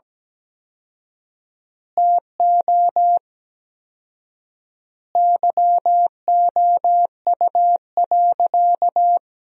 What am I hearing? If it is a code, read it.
Morse code, heard as a keyed tone, about 17 wpm, 700 Hz. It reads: TO YOU.
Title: 4x Word-Spacing - Sentences from Top 500 Words - 17wpm